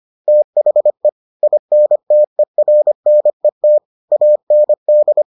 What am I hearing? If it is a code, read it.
Morse code, heard as a keyed tone, about 25 wpm, 600 Hz. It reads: THE INTERNET AND